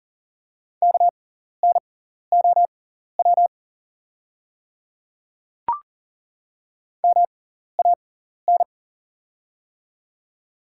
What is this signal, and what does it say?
Morse code, heard as a keyed tone, about 38 wpm, spaced as for 14 wpm, 690 Hz. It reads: KNOW MAN
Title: 6x Character Spacing - Top 300 Words - 40wpm